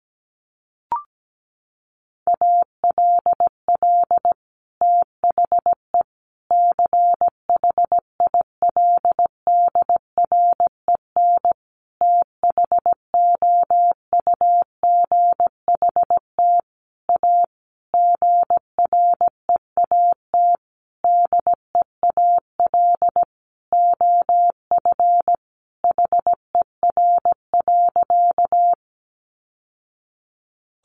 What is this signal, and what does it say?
Morse code, heard as a keyed tone, about 17 wpm, 700 Hz. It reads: ALL THE CHILDREN THOUGHT A GREAT DEAL OF HER.
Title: Sentences from Top 800 Words 17wpm